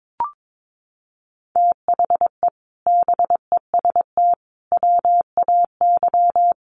Morse code, heard as a keyed tone, about 22 wpm, 700 Hz. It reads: THE BEST WAY
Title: Sets of 4 Words 22wpm